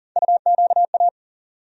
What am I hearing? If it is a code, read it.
Morse code, heard as a keyed tone, about 40 wpm, 700 Hz. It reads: UQA
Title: USA Call Sign Postfixes - 3 Letters 40wpm